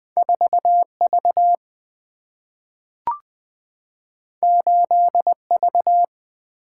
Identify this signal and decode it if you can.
Morse code, heard as a keyed tone, about 20 wpm, 700 Hz. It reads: 4 V 8 V
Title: Single-Letter Single-Number 20wpm